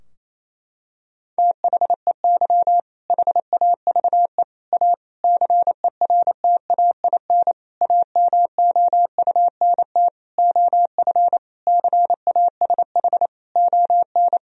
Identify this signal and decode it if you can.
Morse code, heard as a keyed tone, about 28 wpm, 700 Hz. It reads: THEY HAVE A CERTAIN AMOUNT OF CASH ON
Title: Sentences from Top 1000 Words - Review Part 4 - 28wpm